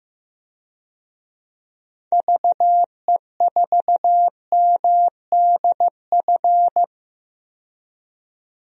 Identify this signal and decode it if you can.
Morse code, heard as a keyed tone, about 15 wpm, 700 Hz. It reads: VE4MDF
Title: DXCC-100 Call Signs - 6 Characters - 15wpm